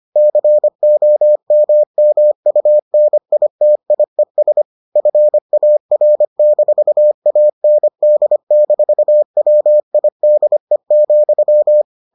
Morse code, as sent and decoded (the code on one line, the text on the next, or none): -.-. --- -- -- ..- -. .. - .. ... / ..-. .- .-. -....- .- -. -.. -....- .-- .. -.. --..--
COMMUNITIES FAR-AND-WIDE,